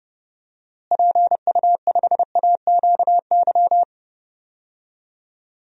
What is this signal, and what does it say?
Morse code, heard as a keyed tone, about 31 wpm, 700 Hz. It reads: PU5AQY
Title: DXCC-100 Call Signs - 6 Characters - Repeated 2x - 30wpm